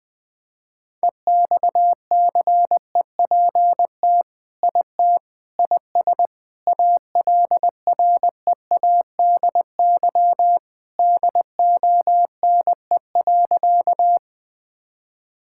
Morse code, heard as a keyed tone, about 20 wpm, 700 Hz. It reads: EXCEPT IT IS ALREADY DONE.
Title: Sentences from Top 1600 Words 20wpm